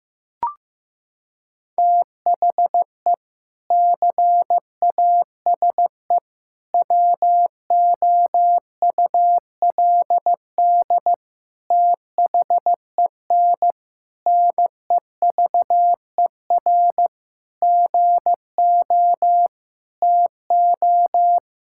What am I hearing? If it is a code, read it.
Morse code, heard as a keyed tone, about 15 wpm, 700 Hz. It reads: THE CASE WOULD THEN NEVER GO TO
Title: Sentences from Top 1600 Words 15wpm